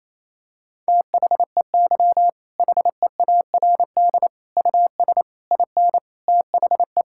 Morse code, heard as a keyed tone, about 28 wpm, 700 Hz. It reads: THEY HEARD US IN THE
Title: Sentences from Top 1000 Words - Review Part 5 - 28wpm